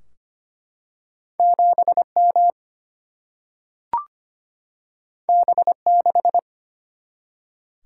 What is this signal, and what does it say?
Morse code, heard as a keyed tone, about 25 wpm, 700 Hz. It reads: 7M B6
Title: Single-Letter Single-Number 25wpm